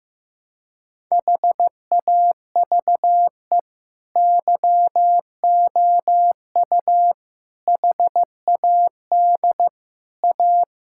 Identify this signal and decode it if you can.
Morse code, heard as a keyed tone, about 15 wpm, 700 Hz. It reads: HAVE YOU HAD A
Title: Sentences from Top 1000 Words - Review Part 6 - 15wpm